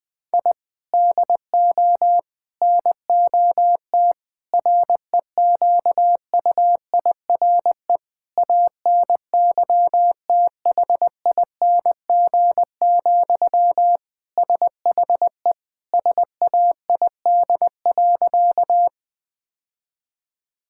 Morse code, heard as a keyed tone, about 20 wpm, 700 Hz. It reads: I DO NOT REQUIRE ANYTHING, SHE SAID.
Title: Sentences from Top 1000 Words - Review Part 6 - 20wpm